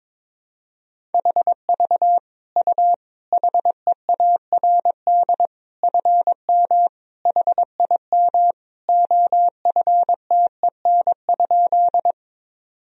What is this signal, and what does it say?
Morse code, heard as a keyed tone, about 22 wpm, 700 Hz. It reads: HV U HEARD FM HIM OFTEN?
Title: Sentences from Top 400 Words and Abbreviations 22wpm